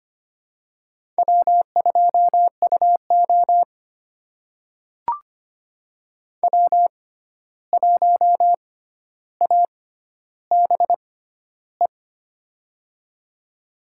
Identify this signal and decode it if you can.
Morse code, heard as a keyed tone, about 25 wpm, spaced as for 12 wpm, 700 Hz. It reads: W2UO W1ABE
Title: USA Call Signs - 6x Character Spacing - 25wpm